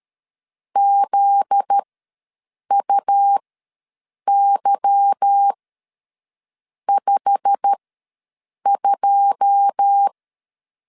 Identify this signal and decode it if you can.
Morse code, heard as a keyed tone, about 13 wpm, spaced as for 6 wpm, 800 Hz. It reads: ZUY52
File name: LEZIONI / 10 SIGNS / 10A SLASH ws 12wpm 3443